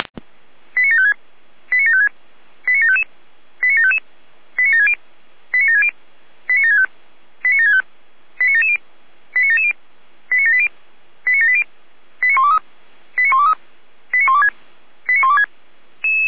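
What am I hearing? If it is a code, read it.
Five-tone selective call (ZVEI), heard as quick runs of five tones, tones of about 70 ms. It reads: 89756 89756 89755 89755 89760 89760 89765 89765 89770 89770 89780 89780 89123 89123 89126 89126
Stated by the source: SDR receiver recording of ZVEI five-tone selective call, code 89735.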